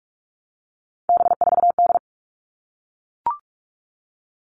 Morse code, heard as a keyed tone, about 44 wpm, 700 Hz. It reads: B4D
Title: All Cut Numbers - 3-Digit 45wpm